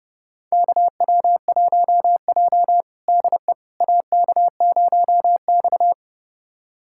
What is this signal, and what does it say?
Morse code, heard as a keyed tone, about 30 wpm, 700 Hz. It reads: KW1J DE AK0X
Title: USA Callsign DE Callsign - 4 Characters - 30wpm